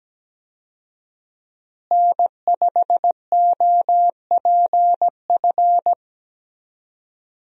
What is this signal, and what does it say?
Morse code, heard as a keyed tone, about 17 wpm, 700 Hz. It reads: N5OPF
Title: USA Call Signs - 5 Characters - Repeated 2x - 17wpm